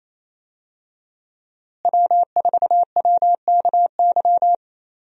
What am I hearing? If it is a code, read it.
Morse code, heard as a keyed tone, about 28 wpm, 700 Hz. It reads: W4WKY
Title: USA Call Signs - 5 Characters - Repeated 2x - 28wpm